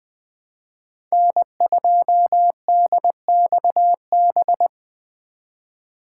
Morse code, heard as a keyed tone, about 21 wpm, 700 Hz. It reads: N2DXB